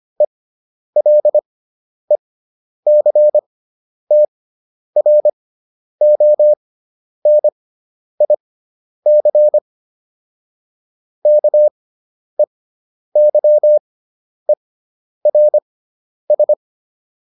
Morse code, heard as a keyed tone, about 25 wpm, spaced as for 10 wpm, 600 Hz. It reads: ELECTRONIC KEYERS